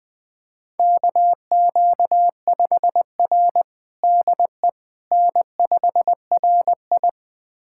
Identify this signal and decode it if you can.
Morse code, heard as a keyed tone, about 20 wpm, 700 Hz. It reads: KQ5R DE N5RI
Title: USA Callsign DE Callsign - 4 Characters - 20wpm